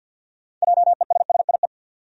Morse code, heard as a keyed tone, about 52 wpm, 700 Hz. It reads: JESSIE
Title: USA Common Names - 50wpm